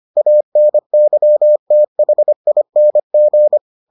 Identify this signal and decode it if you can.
Morse code, heard as a keyed tone, about 25 wpm, 600 Hz. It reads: ANYTHING